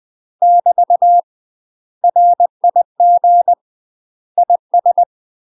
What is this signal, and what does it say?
Morse code, heard as a keyed tone, about 20 wpm, 700 Hz. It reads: <BT> RIG IS
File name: UK MASTER QSOs TRAINING / 20WPM / UK MASTER QSOs 0055